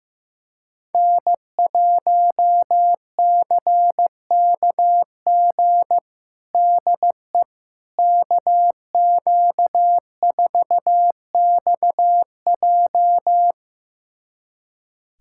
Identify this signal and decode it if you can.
Morse code, heard as a keyed tone, about 15 wpm, 700 Hz. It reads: N1CKG DE KQ4XJ
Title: USA Callsign DE Callsign - 5 Characters - 15wpm